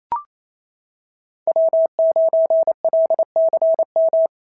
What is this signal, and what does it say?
Morse code, heard as a keyed tone, about 28 wpm, 650 Hz. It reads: W9LCM
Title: SOTA QSO Practice 28wpm